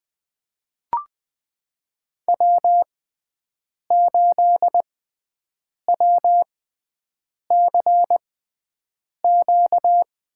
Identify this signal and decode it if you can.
Morse code, heard as a keyed tone, about 21 wpm, spaced as for 7 wpm, 700 Hz. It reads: W8WCQ
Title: USA Call Signs - 6x Character Spacing - 20wpm